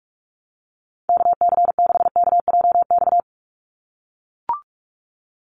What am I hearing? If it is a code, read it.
Morse code, heard as a keyed tone, about 44 wpm, 700 Hz. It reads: KC6KJX
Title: USA Call Signs 45wpm